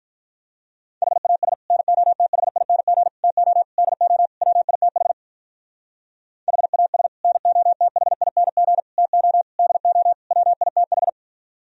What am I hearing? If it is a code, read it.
Morse code, heard as a keyed tone, about 48 wpm, 700 Hz. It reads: HAS NOTHING TO DO WITH HAS NOTHING TO DO WITH